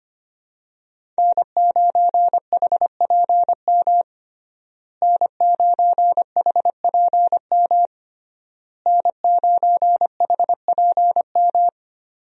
Morse code, heard as a keyed tone, about 25 wpm, 700 Hz. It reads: N9HPM N9HPM N9HPM
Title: USA Call Signs - 5 Characters - Repeated 3x - 25wpm